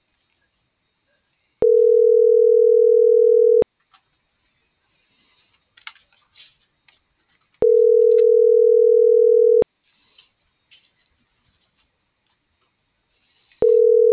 Ringback tone (North America), 440 + 480 Hz.